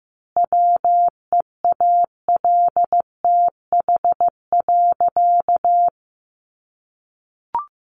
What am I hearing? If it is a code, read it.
Morse code, heard as a keyed tone, about 15 wpm, 700 Hz. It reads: WEALTH.
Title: Sentences from Top 1300 Words 15wpm